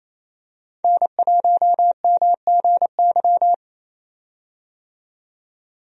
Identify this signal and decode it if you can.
Morse code, heard as a keyed tone, about 28 wpm, 700 Hz. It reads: N1MGY
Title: USA Call Signs - 5 Characters - 28wpm